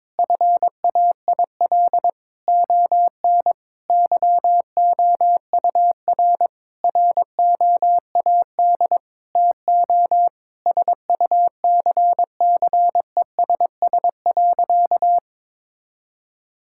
Morse code, heard as a keyed tone, about 22 wpm, 700 Hz. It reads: FAIL ON YOUR ROAD TO SUCCESS.
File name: Difficult Sentences with Hints - 22wpm